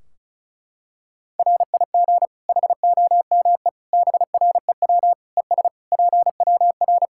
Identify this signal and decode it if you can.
Morse code, heard as a keyed tone, about 36 wpm, 700 Hz. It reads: RIG HOME BREW ES PWR
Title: Practice QSO - Exchange 2 - Info - 35wpm